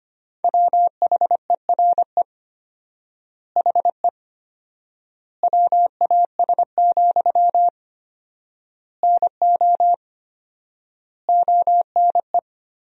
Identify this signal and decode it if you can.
Morse code, heard as a keyed tone, about 25 wpm, 700 Hz. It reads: WHERE HE WAS, NO ONE
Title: 4x Word-Spacing - Sentences from Top 400 Words - 25wpm